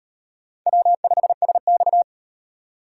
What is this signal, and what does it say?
Morse code, heard as a keyed tone, about 38 wpm, 700 Hz. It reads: W5SX